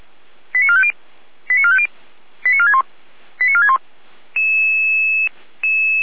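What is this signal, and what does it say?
Five-tone selective call (ZVEI), heard as quick runs of five tones, tones of about 70 ms. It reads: 89450 89450 89461 89461
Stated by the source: ZVEI five-tone call 88383 received on SDR.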